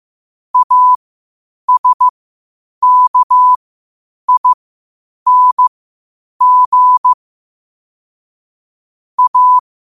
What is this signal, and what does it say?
Morse code, heard as a keyed tone, about 15 wpm, spaced as for 8 wpm, 1000 Hz. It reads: ASKING A
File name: qotdfiles / 8wpm